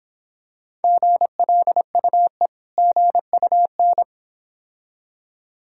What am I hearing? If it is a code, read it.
Morse code, heard as a keyed tone, about 25 wpm, 700 Hz. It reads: GLUE GUN